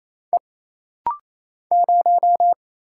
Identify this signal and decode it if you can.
Morse code, heard as a keyed tone, about 28 wpm, 700 Hz. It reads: E 0
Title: Single Letter-Number - Rapid-Fire 28wpm